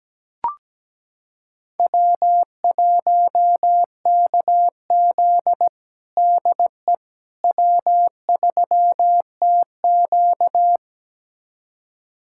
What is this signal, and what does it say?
Morse code, heard as a keyed tone, about 17 wpm, 700 Hz. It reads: W1KZ DE W3TQ
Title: USA Callsign DE Callsign - 4 Characters - 17wpm